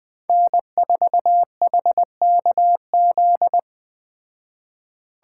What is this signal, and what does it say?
Morse code, heard as a keyed tone, about 20 wpm, 700 Hz. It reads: N4HKZ
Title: Speed Racing - USA Call Signs 17wpm